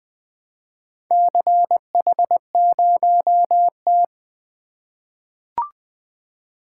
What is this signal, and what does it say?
Morse code, heard as a keyed tone, about 20 wpm, 700 Hz. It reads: CH0T
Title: Four Letter-Number Combinations - 20wpm